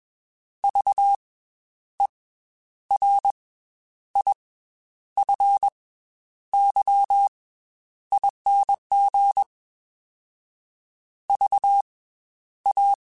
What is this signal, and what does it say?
Morse code, heard as a keyed tone, about 21 wpm, 800 Hz. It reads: V E R I F Y ING V A
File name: Special INGs / Special INGs-mp3s / Slow ing @20